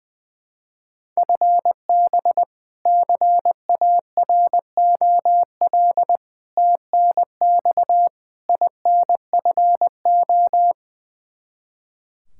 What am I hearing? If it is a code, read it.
Morse code, heard as a keyed tone, about 20 wpm, 700 Hz. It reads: FB CAROL TNX INFO
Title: Practice QSO - Exchange 3 - More Info - 20wpm